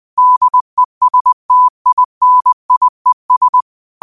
Morse code, heard as a keyed tone, about 20 wpm, 1000 Hz. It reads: DESTINIES
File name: qotdfiles / 20wpm